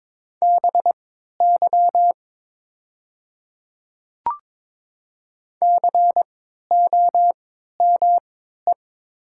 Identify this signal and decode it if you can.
Morse code, heard as a keyed tone, about 22 wpm, spaced as for 12 wpm, 700 Hz. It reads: BY COME
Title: 3x Character Spacing - Top 200 Words - 22wpm